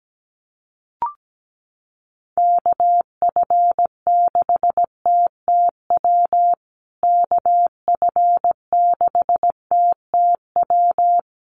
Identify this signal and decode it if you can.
Morse code, heard as a keyed tone, about 17 wpm, 700 Hz. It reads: KF6TTW KF6TTW
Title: USA Call Signs Repeated 3x 17wpm